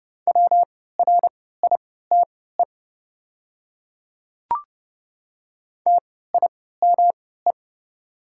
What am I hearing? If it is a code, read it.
Morse code, heard as a keyed tone, about 30 wpm, spaced as for 17 wpm, 700 Hz. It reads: WRITE TIME
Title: 3x Character Spacing - Top 100 Words - 30wpm